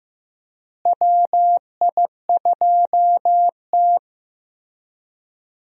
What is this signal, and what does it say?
Morse code, heard as a keyed tone, about 15 wpm, 700 Hz. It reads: WI2T
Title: USA Call Signs - 4 Characters - Repeated 3x - 15wpm